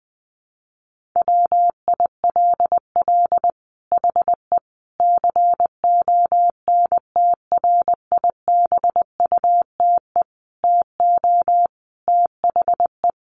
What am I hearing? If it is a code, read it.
Morse code, heard as a keyed tone, about 20 wpm, 700 Hz. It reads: WILL HE CONTRIBUTE TO THE
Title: Sentences from Top 1400 Words 20wpm